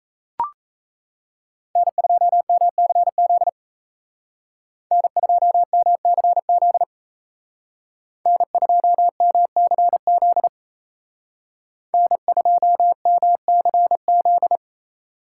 Morse code, heard as a keyed tone, about 33 wpm, 700 Hz. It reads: N2MCZ N2MCZ N2MCZ N2MCZ
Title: Speed Racing - USA 1x3 Call Signs 28wpm